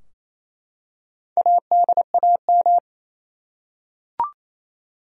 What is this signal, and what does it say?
Morse code, heard as a keyed tone, about 28 wpm, 700 Hz. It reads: ADAM
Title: USA Common Names - Max 4 Letters - 28wpm